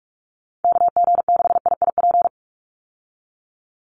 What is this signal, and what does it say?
Morse code, heard as a keyed tone, about 44 wpm, 700 Hz. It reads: KG6IIP
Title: Call Signs Repeated 2x - 45wpm